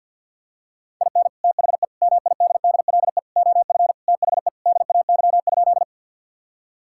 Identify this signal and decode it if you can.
Morse code, heard as a keyed tone, about 51 wpm, 700 Hz. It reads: IN THE MIDDLE OF THE DAY?
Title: Sentences from Top 1000 Words - Review Part 6 - 50wpm